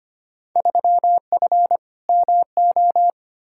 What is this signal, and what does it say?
Morse code, heard as a keyed tone, about 25 wpm, 700 Hz. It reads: 3F MO
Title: Field Day Exchanges Encore - 25wpm